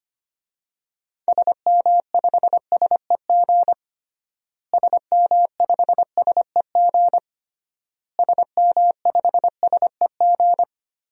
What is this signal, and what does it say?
Morse code, heard as a keyed tone, about 25 wpm, 700 Hz. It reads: SM5SEG SM5SEG SM5SEG
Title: DXCC-100 Call Signs - 6 Characters - Repeated 3x - 25wpm